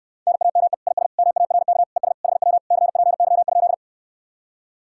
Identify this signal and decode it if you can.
Morse code, heard as a keyed tone, about 69 wpm, 700 Hz. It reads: NAME IS MARK ES HW CPY?